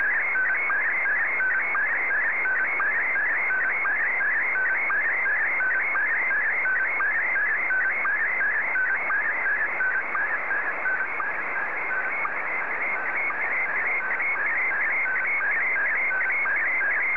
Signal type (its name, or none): SSTV (slow-scan television)